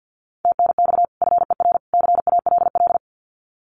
Morse code, heard as a keyed tone, about 48 wpm, 700 Hz. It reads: TNX FER CALL
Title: Common QSO Phrases 50wpm